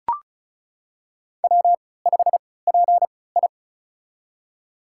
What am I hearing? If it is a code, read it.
Morse code, heard as a keyed tone, about 36 wpm, spaced as for 20 wpm, 700 Hz. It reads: W5PI
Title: USA Call Signs - 4 Characters - 3x Character Spacing - 35wpm